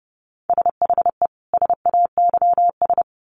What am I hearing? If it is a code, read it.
Morse code, heard as a keyed tone, about 30 wpm, 700 Hz. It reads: SHE SAYS
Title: Sets of 2 Words 30wpm